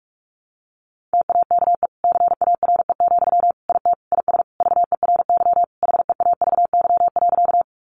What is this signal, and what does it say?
Morse code, heard as a keyed tone, about 44 wpm, 700 Hz. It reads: TAKE CARE, IT IS VERY HEAVY.
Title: Sentences from Top 1100 Words 45wpm